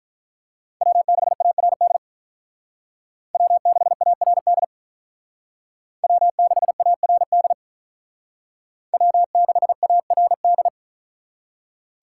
Morse code, heard as a keyed tone, about 43 wpm, 700 Hz. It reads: W6ARD W6ARD W6ARD W6ARD
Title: Speed Racing - USA 1x3 Call Signs 35wpm